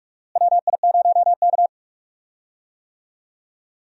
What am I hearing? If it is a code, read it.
Morse code, heard as a keyed tone, about 43 wpm, 700 Hz. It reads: WI0K
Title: USA Call Signs - 4 Characters - Repeated 2x - 45wpm